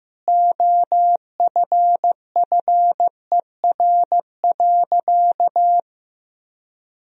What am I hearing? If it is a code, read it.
Morse code, heard as a keyed tone, about 15 wpm, 700 Hz. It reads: OFFER.
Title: Sentences from Top 1000 Words - Review Part 4 - 15wpm